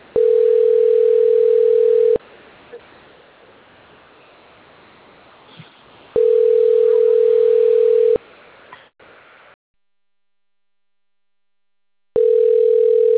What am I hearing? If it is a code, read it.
Ringback tone (North America), 440 + 480 Hz.